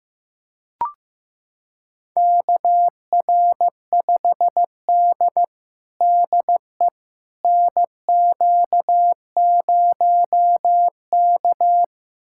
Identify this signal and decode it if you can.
Morse code, heard as a keyed tone, about 15 wpm, 700 Hz. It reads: KR5D DE NQ0K
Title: USA Callsign DE Callsign - 4 Characters - 15wpm